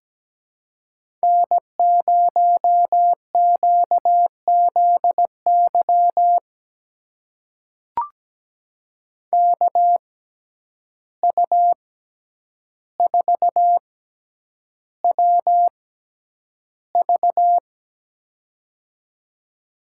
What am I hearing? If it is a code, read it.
Morse code, heard as a keyed tone, about 17 wpm, spaced as for 9 wpm, 700 Hz. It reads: N0QZY KU4WV
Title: USA Call Signs - 5 Characters - 6x Character Spacing - 17wpm